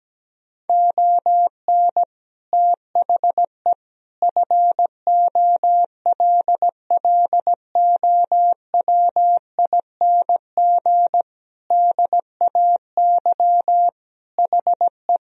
Morse code, heard as a keyed tone, about 17 wpm, 700 Hz. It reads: ON THE FOLLOWING DAY HE
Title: Sentences from Top 1000 Words - Review Part 4 - 17wpm